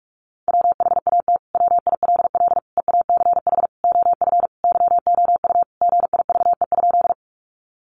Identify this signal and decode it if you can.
Morse code, heard as a keyed tone, about 44 wpm, 700 Hz. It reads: WHAT WILL EACH OF YOU GIVE?
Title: Sentences from Top 200 Words 45wpm